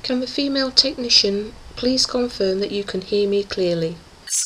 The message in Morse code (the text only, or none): none